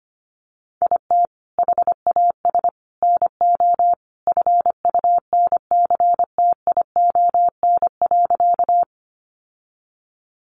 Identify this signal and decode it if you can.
Morse code, heard as a keyed tone, about 25 wpm, 700 Hz. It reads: IT HAS NO FUNCTION.
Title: Sentences from Top 1100 Words 25wpm